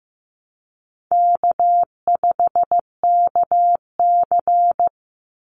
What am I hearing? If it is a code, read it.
Morse code, heard as a keyed tone, about 15 wpm, 700 Hz. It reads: K5KC
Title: Easy Call Signs Repeated 3x - 15wpm